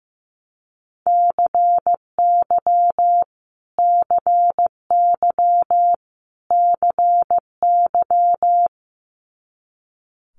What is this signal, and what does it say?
Morse code, heard as a keyed tone, about 15 wpm, 700 Hz. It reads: CY CY CY
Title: Multi-letter Phonemes Repeated 3x 15wpm